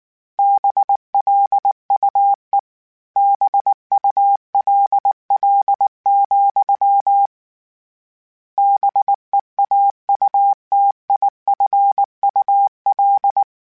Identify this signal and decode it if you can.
Morse code, heard as a keyed tone, about 19 wpm, 800 Hz. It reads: BLUE BULL, BEAUTIFUL
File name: Misc / Misc-mp3s / BIG @18